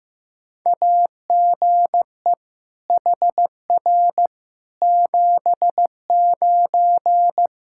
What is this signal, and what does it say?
Morse code, heard as a keyed tone, about 15 wpm, 700 Hz. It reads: AGE HR 79
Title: Practice QSO - Exchange 3 - More Info - 15wpm